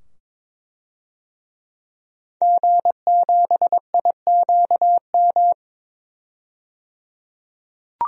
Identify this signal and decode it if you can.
Morse code, heard as a keyed tone, about 22 wpm, 700 Hz. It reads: G7IQM
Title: DXCC-100 Call Signs - 5 Characters - 22wpm